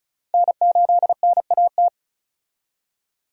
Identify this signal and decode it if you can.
Morse code, heard as a keyed tone, about 36 wpm, 700 Hz. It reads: N8NAT
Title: USA Call Signs - 5 Characters - 35wpm